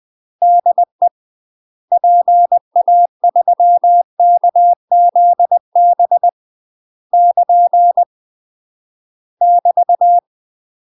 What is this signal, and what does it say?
Morse code, heard as a keyed tone, about 20 wpm, 700 Hz. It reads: DE PA3KZB <KN> <BT>
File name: UK MASTER QSOs TRAINING / 20WPM / UK MASTER QSOs 0038